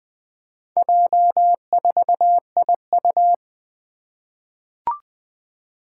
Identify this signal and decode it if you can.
Morse code, heard as a keyed tone, about 20 wpm, 700 Hz. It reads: J4IU
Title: Four Challenging Letter-Number Combinations 20wpm